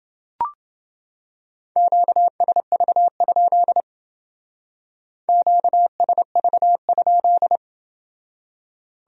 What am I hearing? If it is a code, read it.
Morse code, heard as a keyed tone, about 28 wpm, 700 Hz. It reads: QSV? QSV?